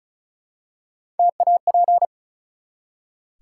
Morse code, heard as a keyed tone, about 36 wpm, 700 Hz. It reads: TAP
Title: Three Letter Words - 35wpm